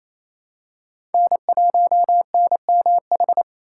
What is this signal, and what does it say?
Morse code, heard as a keyed tone, about 28 wpm, 700 Hz. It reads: N1NMH
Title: USA Call Signs - 5 Characters - 4x Character Spacing - 28wpm